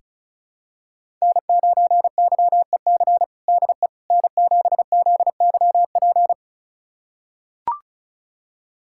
Morse code, heard as a keyed tone, about 34 wpm, 690 Hz. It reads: N9YEC DE N7ZYP
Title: USA Callsign DE Callsign - 5 Characters - 35wpm